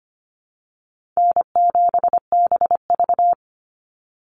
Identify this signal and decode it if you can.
Morse code, heard as a keyed tone, about 25 wpm, 700 Hz. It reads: N7BV
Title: Easy Call Signs Repeated 3x - 25wpm